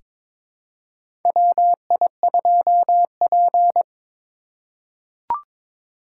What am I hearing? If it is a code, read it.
Morse code, heard as a keyed tone, about 23 wpm, 700 Hz. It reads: WI2P
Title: USA Call Signs - 4 Characters - 3x Character Spacing - 22wpm